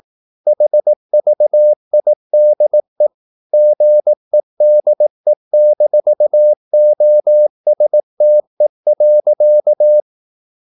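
Morse code, heard as a keyed tone, about 18 wpm, 600 Hz. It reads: HVIDE GEDE-OSTE.
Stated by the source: Slow (QRS) Morse practice at 18 wpm, CW med Gnister.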